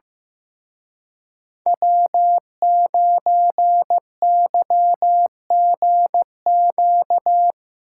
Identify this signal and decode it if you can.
Morse code, heard as a keyed tone, about 16 wpm, 700 Hz. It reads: W9YGQ
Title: Speed Racing - USA 1x3 Call Signs 15wpm